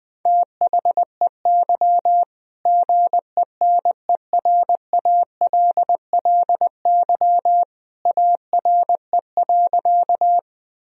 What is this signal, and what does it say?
Morse code, heard as a keyed tone, about 20 wpm, 700 Hz. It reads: THEY GENERALLY ARE.
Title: Sentences from Top 1000 Words - Review Part 4 - 20wpm